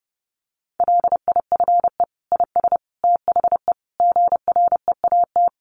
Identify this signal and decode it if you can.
Morse code, heard as a keyed tone, about 30 wpm, 700 Hz. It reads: LIFE IS THE GREAT
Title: Sentences from Top 900 Words 30wpm